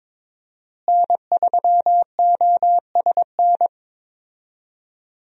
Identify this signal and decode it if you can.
Morse code, heard as a keyed tone, about 22 wpm, 700 Hz. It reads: N3OSN
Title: USA Call Signs - 5 Characters - Repeated 2x - 22wpm